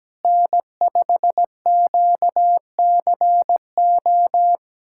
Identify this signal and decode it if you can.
Morse code, heard as a keyed tone, about 17 wpm, 700 Hz. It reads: N5QCO
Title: USA Callsign DE Callsign - 5 Characters - 17wpm